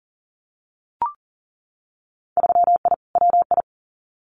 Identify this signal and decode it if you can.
Morse code, heard as a keyed tone, about 40 wpm, 700 Hz. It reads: 3I WI
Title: Winter Field Day Exchanges 40wpm